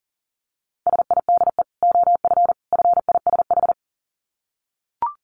Morse code, heard as a keyed tone, about 40 wpm, 700 Hz. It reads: SIDE OF FISH
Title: Dit Challenge 40wpm